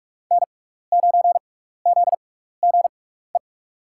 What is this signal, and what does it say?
Morse code, heard as a keyed tone, about 44 wpm, spaced as for 15 wpm, 690 Hz. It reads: N9ZGE
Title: USA Call Signs - 6x Character Spacing - 45wpm